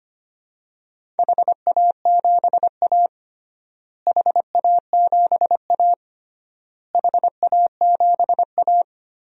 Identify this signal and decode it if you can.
Morse code, heard as a keyed tone, about 25 wpm, 700 Hz. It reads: HA7A HA7A HA7A
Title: DXCC-100 Call Signs - 4 Characters - Repeated 3x - 25wpm